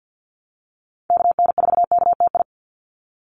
Morse code, heard as a keyed tone, about 48 wpm, 700 Hz. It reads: KN4KTI
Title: USA Call Signs 50wpm